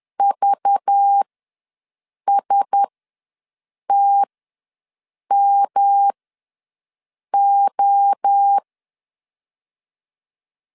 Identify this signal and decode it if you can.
Morse code, heard as a keyed tone, about 10 wpm, spaced as for 6 wpm, 800 Hz. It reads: VSTMO
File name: LEZIONI / 5 LBVC / 5 LBV ws 10wpm 3645